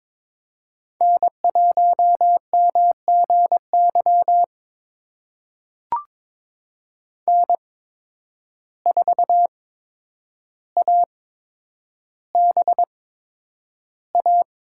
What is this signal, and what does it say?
Morse code, heard as a keyed tone, about 22 wpm, spaced as for 9 wpm, 700 Hz. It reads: N1MGY N4ABA
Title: USA Call Signs - 5 Characters - 8x Character Spacing - 22wpm